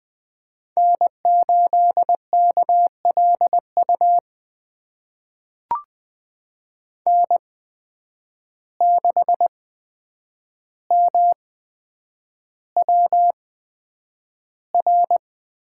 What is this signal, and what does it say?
Morse code, heard as a keyed tone, about 20 wpm, spaced as for 9 wpm, 700 Hz. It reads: N8KLU N6MWR
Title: USA Call Signs - 5 Characters - 8x Character Spacing - 20wpm